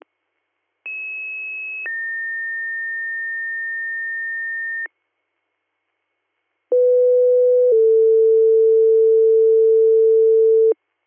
Two-tone page, heard as a one-second tone, then a three-second tone, about 2600 then 1800 Hz, about 500 then 450 Hz. It is heard from 0.9 to 10.7 s.